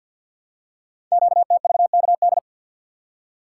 Morse code, heard as a keyed tone, about 49 wpm, 700 Hz. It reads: QTVKD
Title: Code Groups - 5 Letter:Numbers - 50wpm